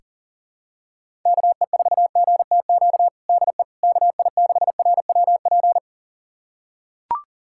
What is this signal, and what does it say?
Morse code, heard as a keyed tone, about 40 wpm, 700 Hz. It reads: KE4GTQ DE KI6RWP
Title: USA Callsign DE Callsign - 6 Characters - 40wpm